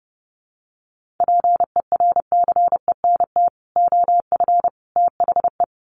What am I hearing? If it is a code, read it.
Morse code, heard as a keyed tone, about 30 wpm, 700 Hz. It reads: PERCENT OF THE
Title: Sets of 3 Words 30wpm